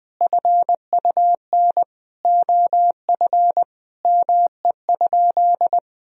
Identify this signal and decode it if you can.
Morse code, heard as a keyed tone, about 20 wpm, 700 Hz. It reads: FUN OF ME?
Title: Sentences from Top 1000 Words - Review Part 4 - 20wpm